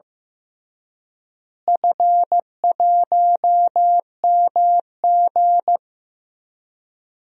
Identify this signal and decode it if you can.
Morse code, heard as a keyed tone, about 15 wpm, 700 Hz. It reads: F1MG
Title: DXCC-100 Call Signs - 4 Characters - Repeated 3x - 15wpm